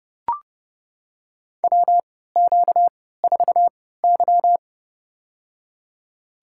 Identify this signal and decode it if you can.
Morse code, heard as a keyed tone, about 30 wpm, spaced as for 17 wpm, 700 Hz. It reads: WQ4Y
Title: USA Call Signs - 4 Characters - 3x Character Spacing - 30wpm